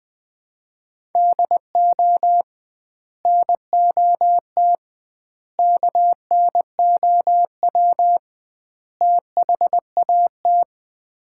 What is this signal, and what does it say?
Morse code, heard as a keyed tone, about 20 wpm, 700 Hz. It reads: DO NOT KNOW THAT
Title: 2x Word-Spacing - Sets of 4 Words - 20wpm